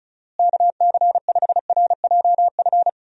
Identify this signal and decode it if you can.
Morse code, heard as a keyed tone, about 36 wpm, 700 Hz. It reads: KC5RJF
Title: USA Call Signs - 6 Characters - Repeated 2x - 35wpm